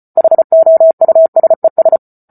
Morse code, heard as a keyed tone, about 34 wpm, 650 Hz. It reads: HOUSES